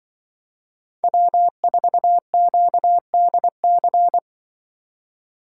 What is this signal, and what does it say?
Morse code, heard as a keyed tone, about 24 wpm, 700 Hz. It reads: W4QDC
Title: Speed Racing - USA 1x3 Call Signs 20wpm